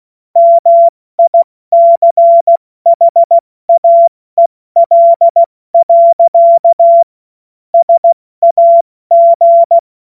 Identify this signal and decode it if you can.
Morse code, heard as a keyed tone, about 16 wpm, 680 Hz. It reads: MICHAEL. SAG